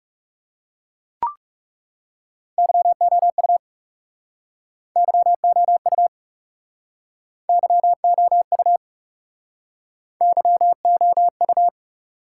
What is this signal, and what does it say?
Morse code, heard as a keyed tone, about 42 wpm, 700 Hz. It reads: YOU YOU YOU YOU